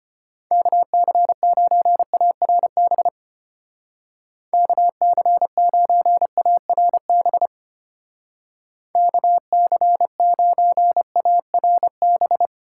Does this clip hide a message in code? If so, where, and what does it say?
Yes: Morse code from 0.5 to 12.5 s, heard as a keyed tone, about 33 wpm, 700 Hz. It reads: KC9ARB KC9ARB KC9ARB